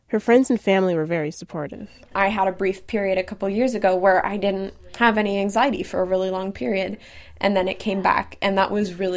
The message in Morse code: none

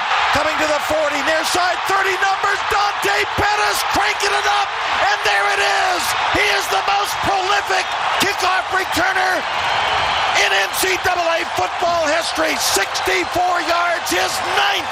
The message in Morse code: none

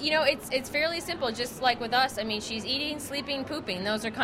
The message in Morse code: none

